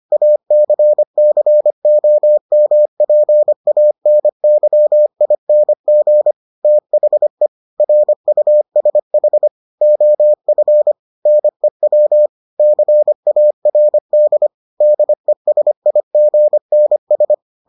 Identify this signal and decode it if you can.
Morse code, heard as a keyed tone, about 25 wpm, 600 Hz. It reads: ACCOMPANYING THE RUSH OF NEW CARD DESIGNS